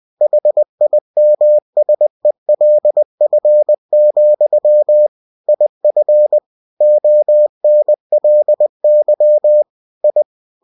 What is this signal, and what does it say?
Morse code, heard as a keyed tone, about 20 wpm, 600 Hz. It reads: HIMSELF, IF ONLY I
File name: cw-books / winnie-the-pooh / winnie-20wpm-page-0036